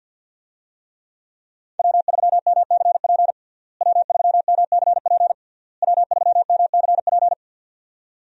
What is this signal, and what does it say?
Morse code, heard as a keyed tone, about 52 wpm, 700 Hz. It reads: W3MCP W3MCP W3MCP
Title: USA Call Signs - 5 Characters - Repeated 3x - 50wpm